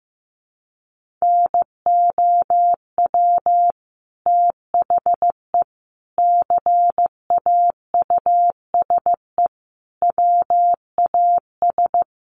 Morse code, heard as a keyed tone, about 15 wpm, 700 Hz. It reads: NOW THE CAUSE WAS
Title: Sentences from Top 1500 Words 15wpm